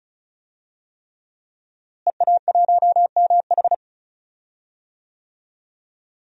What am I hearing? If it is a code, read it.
Morse code, heard as a keyed tone, about 36 wpm, 700 Hz. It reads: EA1MH
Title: DXCC-100 Call Signs - 5 Characters - Repeated 3x - 35wpm